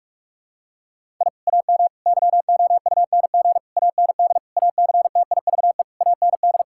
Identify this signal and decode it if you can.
Morse code, heard as a keyed tone, about 46 wpm, 700 Hz. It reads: I AM YOUNG AND ACTIVE AND
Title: Sentences from Top 1000 Words - Review Part 6 - 45wpm